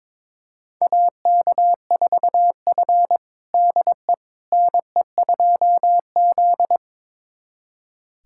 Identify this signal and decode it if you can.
Morse code, heard as a keyed tone, about 22 wpm, 700 Hz. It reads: AK4F DE NE2Z
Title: USA Callsign DE Callsign - 4 Characters - 22wpm